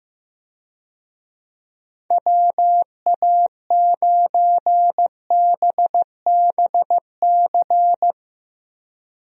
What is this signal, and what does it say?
Morse code, heard as a keyed tone, about 15 wpm, 700 Hz. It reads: WA9BBC
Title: USA Call Signs - 6 Characters - Repeated 2x - 15wpm